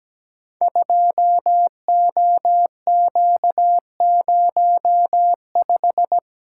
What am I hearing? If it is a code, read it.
Morse code, heard as a keyed tone, about 17 wpm, 700 Hz. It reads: 2OQ05
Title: Code Groups - 5 Letter:Numbers - 17wpm